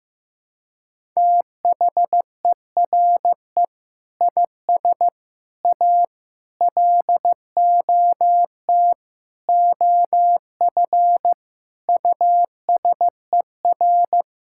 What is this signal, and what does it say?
Morse code, heard as a keyed tone, about 15 wpm, 700 Hz. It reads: THERE IS A LOT OF USER
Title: Sentences from Top 1000 Words - Review Part 6 - 15wpm